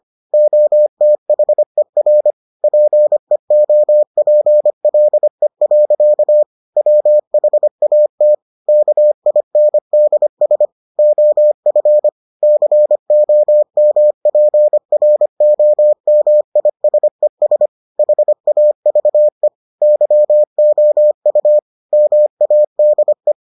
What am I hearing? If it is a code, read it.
Morse code, heard as a keyed tone, about 25 wpm, 600 Hz. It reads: OTHER PEOPLE. WHAT KINDS OF COMPROMISES HAVE YOU MADE